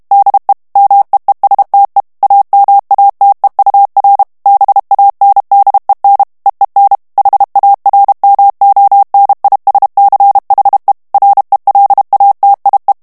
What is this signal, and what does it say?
Morse code, heard as a keyed tone, about 32 wpm, 800 Hz. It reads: DE MEESTE AMATEUR BANDEN EEN HARMONISCHE RELATIE
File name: HOLLANDAIS texte clair 1 vitesse 30 mots minute